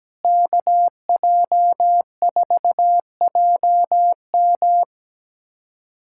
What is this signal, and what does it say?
Morse code, heard as a keyed tone, about 17 wpm, 700 Hz. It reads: KJ4JM
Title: USA Callsign DE Callsign - 5 Characters - 17wpm